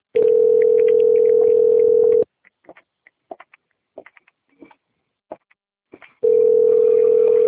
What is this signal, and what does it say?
Ringback tone (North America), 440 + 480 Hz.